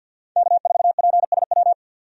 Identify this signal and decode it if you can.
Morse code, heard as a keyed tone, about 47 wpm, 700 Hz. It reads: K4PSW
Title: USA Call Signs - 5 Characters - 50wpm